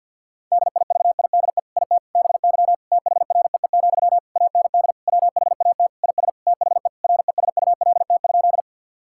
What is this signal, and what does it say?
Morse code, heard as a keyed tone, about 51 wpm, 700 Hz. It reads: DIVIDE IT BY THREE, AND WHAT IS THE RESULT?
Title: Sentences from Top 1600 Words 50wpm